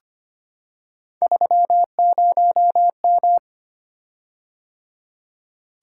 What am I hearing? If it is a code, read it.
Morse code, heard as a keyed tone, about 25 wpm, 700 Hz. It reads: 30M
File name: Three Letter-Number Combinations - 25wpm